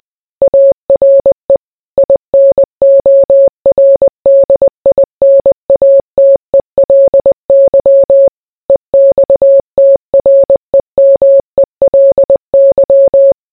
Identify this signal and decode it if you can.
Morse code, heard as a keyed tone, about 20 wpm, 550 Hz. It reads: ARE INORDINATELY EXTREMELY